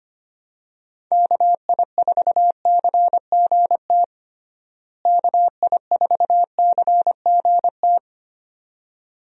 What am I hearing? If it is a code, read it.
Morse code, heard as a keyed tone, about 25 wpm, 700 Hz. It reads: KI4CGT KI4CGT
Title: USA Call Signs - 6 Characters - Repeated 2x - 25wpm